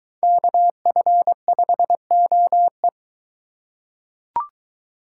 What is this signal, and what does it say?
Morse code, heard as a keyed tone, about 22 wpm, 700 Hz. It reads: KF5OE